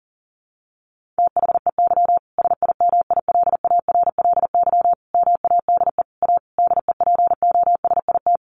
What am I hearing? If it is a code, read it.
Morse code, heard as a keyed tone, about 37 wpm, 700 Hz. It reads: THEY SIMILARLY MADE A DEPOSIT
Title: Sentences from Top 1400 Words 40wpm